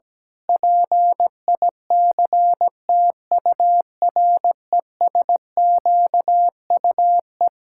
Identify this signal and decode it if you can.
Morse code, heard as a keyed tone, about 17 wpm, 700 Hz. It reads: PICTURESQUE
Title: Pangram Challenge v2 - 17wpm